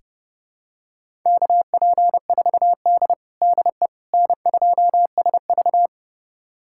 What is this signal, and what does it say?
Morse code, heard as a keyed tone, about 30 wpm, 700 Hz. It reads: KP4D DE N2SV
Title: USA Callsign DE Callsign - 4 Characters - 30wpm